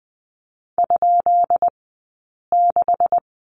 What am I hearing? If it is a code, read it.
Morse code, heard as a keyed tone, about 20 wpm, spaced as for 8 wpm, 700 Hz. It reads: ?6
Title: 171 Lesson 28d - Number 9 - Two Characters 20wpm